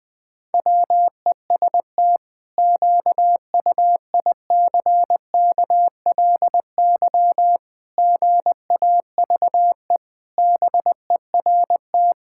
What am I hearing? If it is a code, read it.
Morse code, heard as a keyed tone, about 19 wpm, 700 Hz. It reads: WEST QUICKLY GAVE BERT
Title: Pangram Challenge v2 - 20wpm